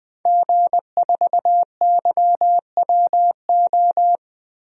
Morse code, heard as a keyed tone, about 19 wpm, 700 Hz. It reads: G4YWO
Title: DXCC-100 Call Signs - 5 Characters - Repeated 3x - 20wpm